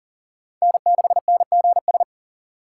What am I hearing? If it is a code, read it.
Morse code, heard as a keyed tone, about 40 wpm, 700 Hz. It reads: N6NGS
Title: USA Call Signs - 5 Characters - 2x Character Spacing - 40wpm